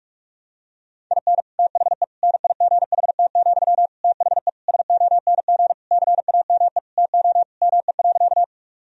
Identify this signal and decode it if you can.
Morse code, heard as a keyed tone, about 46 wpm, 700 Hz. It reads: IN THE NIGHT, THE SONG CAME TO ME.
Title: Sentences from Top 1000 Words - Review Part 2 - 45wpm